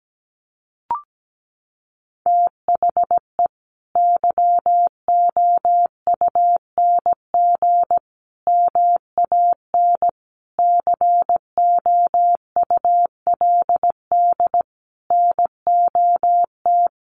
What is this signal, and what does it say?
Morse code, heard as a keyed tone, about 17 wpm, 700 Hz. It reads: THE YOUNG MAN COULD NOT
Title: Sentences from Top 1400 Words 17wpm